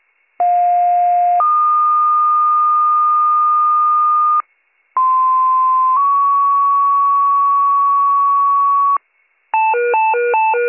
Two-tone page, heard as a one-second tone, then a three-second tone, about 710 then 1220 Hz, about 1040 then 1120 Hz. It is heard from 0.4 to 9.0 s.